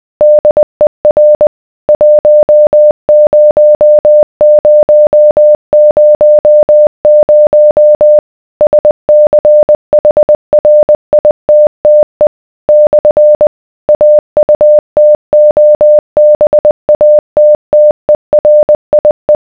Morse code, heard as a keyed tone, about 20 wpm, 600 Hz. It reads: DER 10000 SCHRITTE / AUTOBATTERIE